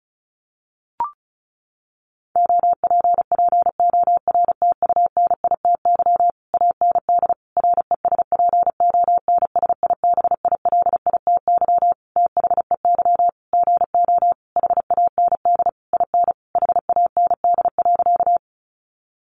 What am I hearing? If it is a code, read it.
Morse code, heard as a keyed tone, about 34 wpm, 700 Hz. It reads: OPPORTUNITY AND RESPONSIBILITY THEY GO HAND IN HAND.
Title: Sentences from Top 1200 Words 35wpm